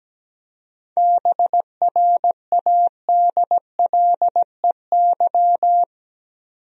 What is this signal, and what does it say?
Morse code, heard as a keyed tone, about 17 wpm, 700 Hz. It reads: BRADLEY
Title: USA Common Names - 17wpm